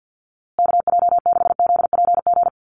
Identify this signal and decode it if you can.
Morse code, heard as a keyed tone, about 50 wpm, 700 Hz. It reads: KJ6ZPG
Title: USA Call Signs Repeated 3x 50wpm